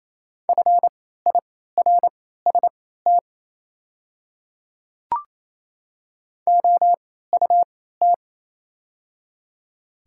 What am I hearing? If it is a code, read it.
Morse code, heard as a keyed tone, about 28 wpm, spaced as for 16 wpm, 700 Hz. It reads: FIRST OUT
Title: 3x Character Spacing - Top 200 Words - 28wpm